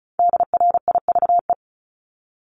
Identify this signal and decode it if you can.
Morse code, heard as a keyed tone, about 36 wpm, 700 Hz. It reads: DRIVE
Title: Multi-letter Phonemes Repeated 3x 35wpm